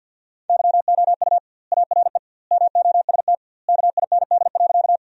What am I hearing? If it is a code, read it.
Morse code, heard as a keyed tone, about 53 wpm, 700 Hz. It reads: YOU ARE MOST KIND.
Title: 2x Word-Spacing - Sentences from Top 200 Words - 50wpm